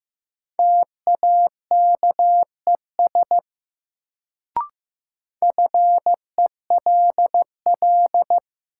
Morse code, heard as a keyed tone, about 15 wpm, 700 Hz. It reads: TAKES FELL
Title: Sets of 5 Words Encore - New Word Drill - 15wpm